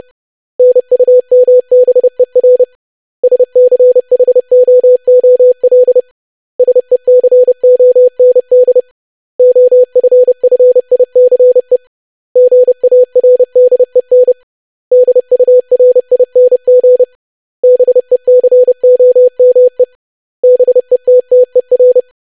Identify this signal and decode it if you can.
Morse code, heard as a keyed tone, about 30 wpm, 500 Hz. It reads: NUMBER SCHOOL SECOND OFFICE GARDEN DURING BECOME BETTER
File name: lesson / web / basic500-25wpm